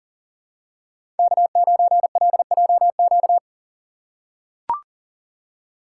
Morse code, heard as a keyed tone, about 40 wpm, 700 Hz. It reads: K9LJQ
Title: USA Call Signs - 6x Character Spacing - 40wpm